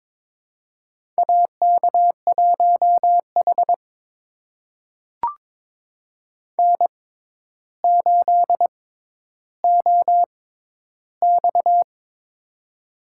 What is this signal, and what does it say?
Morse code, heard as a keyed tone, about 22 wpm, spaced as for 11 wpm, 700 Hz. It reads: AK1H N8OX
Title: USA Call Signs - 4 Characters - 6x Character Spacing - 22wpm